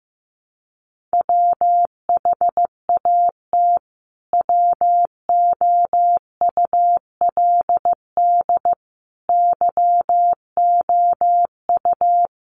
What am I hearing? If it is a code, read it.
Morse code, heard as a keyed tone, about 15 wpm, 700 Hz. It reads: WHAT WOULD YOU
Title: Sentences from Top 1200 Words 15wpm